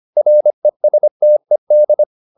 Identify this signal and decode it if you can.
Morse code, heard as a keyed tone, about 25 wpm, 600 Hz. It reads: RESTED
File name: the old man and the sea part 1 25wpm